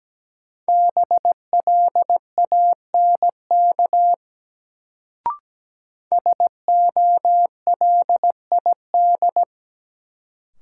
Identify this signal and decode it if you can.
Morse code, heard as a keyed tone, about 17 wpm, 700 Hz. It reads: BLANK SOLID
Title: Sentences from Top 1500 Words - New Word Drill Alt - 17wpm